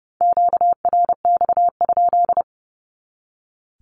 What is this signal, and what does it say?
Morse code, heard as a keyed tone, about 30 wpm, 700 Hz. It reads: QRX?
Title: Q-Codes 30wpm